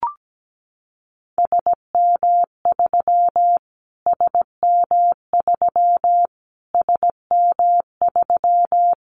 Morse code, heard as a keyed tone, about 17 wpm, 700 Hz. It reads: SM3 SM3 SM3
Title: Callsign Prefixes for Top 25 DXCC - Repeated 3x - 17wpm